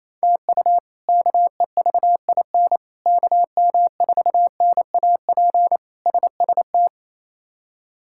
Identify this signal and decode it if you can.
Morse code, heard as a keyed tone, about 28 wpm, 700 Hz. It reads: TU KEVIN KM4NAP SST
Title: K1USN SST QSO Practice - 28wpm